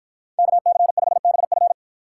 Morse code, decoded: KC5BF

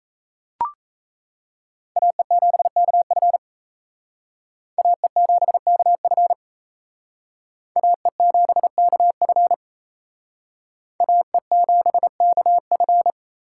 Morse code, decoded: AE7KF AE7KF AE7KF AE7KF

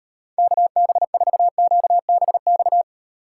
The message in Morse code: KB4QBX

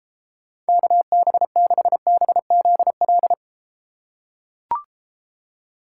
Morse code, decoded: KB6BZL

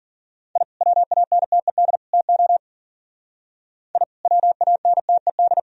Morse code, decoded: I WANTED TO I WANTED

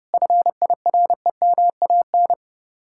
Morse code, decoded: FIREMAN

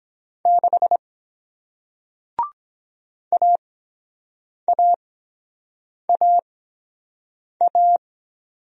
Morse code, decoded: 6 AAAA

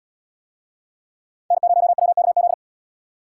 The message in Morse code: N0WKZ

